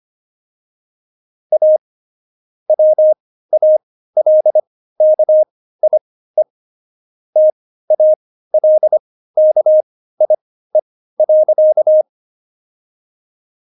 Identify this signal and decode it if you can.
Morse code, heard as a keyed tone, about 25 wpm, spaced as for 15 wpm, 620 Hz. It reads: A WALKIE TALKIE.
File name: Jokes-25-char-15-eff